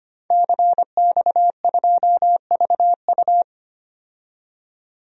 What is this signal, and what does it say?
Morse code, heard as a keyed tone, about 25 wpm, 700 Hz. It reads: CX2VU
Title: DXCC-100 Call Signs - 5 Characters - 25wpm